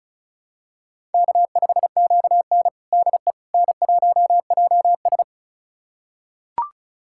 Morse code, heard as a keyed tone, about 36 wpm, 700 Hz. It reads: K5QN DE N1JS
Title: USA Callsign DE Callsign - 4 Characters - 35wpm